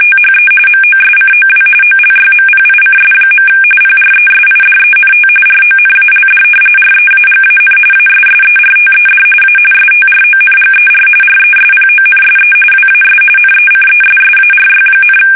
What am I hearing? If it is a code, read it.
RTTY, about 45.45 baud. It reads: THE MOSCOW OLYMPICS. PRESIDENT RONALD REAGAN ASSERTED A STRONGER RESPONSE WAS NECESSARY,
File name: fsk850 / battleshipnj